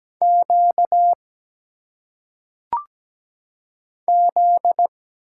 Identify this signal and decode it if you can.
Morse code, heard as a keyed tone, about 17 wpm, 700 Hz. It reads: Q Z
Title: Single Letters - Phonetic Pronunciation 17wpm